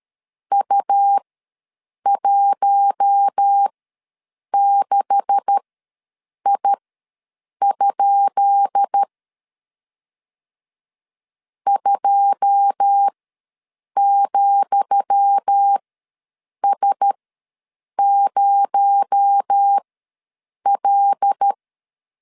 Morse code, heard as a keyed tone, about 13 wpm, spaced as for 7 wpm, 800 Hz. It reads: U16I? 2,S0L
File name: LEZIONI / 10 SIGNS / 10D COMA ws 12wpm 3618